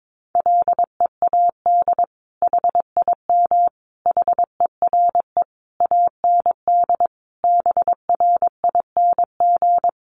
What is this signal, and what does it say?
Morse code, heard as a keyed tone, about 22 wpm, 700 Hz. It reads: LEAD HIM HERE AND BRING
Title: Sentences from Top 1200 Words 22wpm